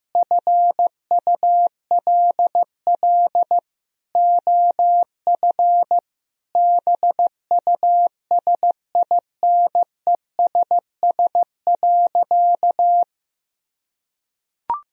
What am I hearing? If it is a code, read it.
Morse code, heard as a keyed tone, about 15 wpm, 700 Hz. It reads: FULL OF BUSINESS.
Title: Sentences from Top 1000 Words - Review Part 3 - 15wpm